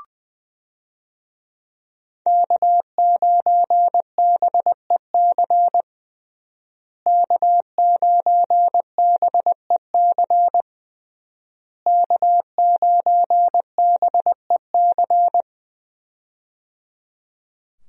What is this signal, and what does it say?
Morse code, heard as a keyed tone, about 20 wpm, 700 Hz. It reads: K9BEC K9BEC K9BEC
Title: USA Call Signs - 5 Characters - Repeated 3x - 20wpm